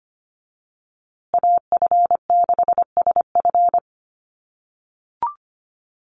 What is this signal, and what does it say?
Morse code, heard as a keyed tone, about 25 wpm, 700 Hz. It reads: AF6SF
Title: USA Call Signs Repeated 3x 25wpm